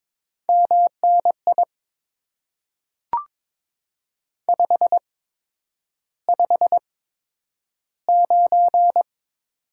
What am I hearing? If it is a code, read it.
Morse code, heard as a keyed tone, about 22 wpm, spaced as for 9 wpm, 700 Hz. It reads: MNI 559
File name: Top 90 QSO Elements - 8x Character Spacing - 22wpm